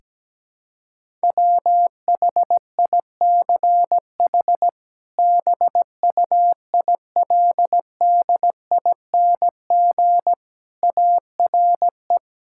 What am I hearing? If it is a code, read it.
Morse code, heard as a keyed tone, about 17 wpm, 700 Hz. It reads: WHICH BUILDING ARE
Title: Sentences from Top 1000 Words - Review Part 5 - 17wpm